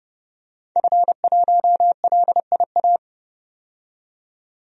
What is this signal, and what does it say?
Morse code, heard as a keyed tone, about 30 wpm, 700 Hz. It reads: F1LIA
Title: DXCC-100 Call Signs - 5 Characters - 30wpm